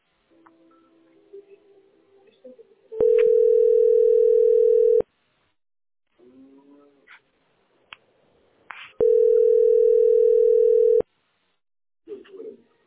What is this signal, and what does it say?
Ringback tone (North America), 440 + 480 Hz.